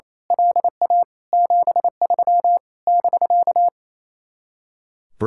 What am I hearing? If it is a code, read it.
Morse code, heard as a keyed tone, about 26 wpm, 700 Hz. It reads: LA 73 <BK>
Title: Speed-Racing Common QSO Phrases - 28wpm